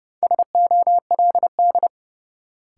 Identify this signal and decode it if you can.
Morse code, heard as a keyed tone, about 30 wpm, 700 Hz. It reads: SOLD
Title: Sentences from Top 1400 Words - New Word Drill - 30wpm